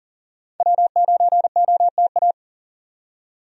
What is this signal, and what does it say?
Morse code, heard as a keyed tone, about 40 wpm, 700 Hz. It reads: W9OTA
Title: USA Call Signs - 5 Characters - 40wpm